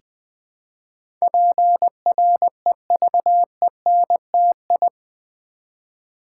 Morse code, heard as a keyed tone, about 20 wpm, 700 Hz. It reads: PREVENTI
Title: Build A Word - 4 Letters At A Time - 10-Letter Words - 20wpm